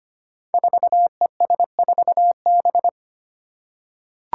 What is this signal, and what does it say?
Morse code, heard as a keyed tone, about 25 wpm, 700 Hz. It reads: 4ES4B
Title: Five Challenging Letter-Number Combinations 25wpm